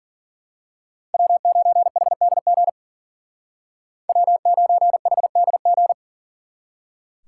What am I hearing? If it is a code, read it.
Morse code, heard as a keyed tone, about 42 wpm, 700 Hz. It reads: W9HDG W9HDG
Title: Speed Racing - USA 1x3 Call Signs 40wpm